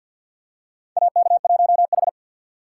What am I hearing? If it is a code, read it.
Morse code, heard as a keyed tone, about 53 wpm, 700 Hz. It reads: AK1H
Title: USA Call Signs - 4 Characters - 3x Character Spacing - 50wpm